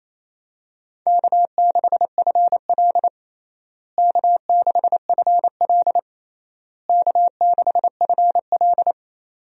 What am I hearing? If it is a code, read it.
Morse code, heard as a keyed tone, about 28 wpm, 700 Hz. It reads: K6FL K6FL K6FL
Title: USA Call Signs - 4 Characters - Repeated 3x - 28wpm